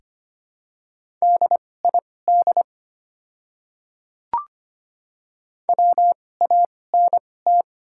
Morse code, heard as a keyed tone, about 25 wpm, spaced as for 18 wpm, 700 Hz. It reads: DID WANT
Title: 2x Character Spacing - Top 200 Words - 25wpm